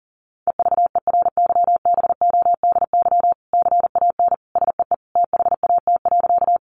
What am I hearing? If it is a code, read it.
Morse code, heard as a keyed tone, about 40 wpm, 700 Hz. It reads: EVERYBODY CAN SEE THAT.
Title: Sentences from Top 1400 Words 40wpm